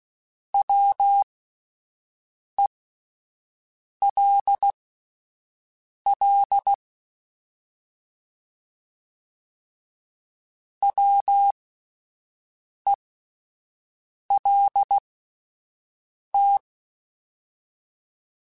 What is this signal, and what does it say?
Morse code, heard as a keyed tone, about 16 wpm, spaced as for 5 wpm, 800 Hz. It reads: WELL WELT
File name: Alphabetical / Alphabetical mp3s / Ws @5